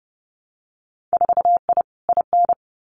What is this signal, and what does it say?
Morse code, heard as a keyed tone, about 30 wpm, 700 Hz. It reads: 4I IN
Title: Winter Field Day Exchanges 30wpm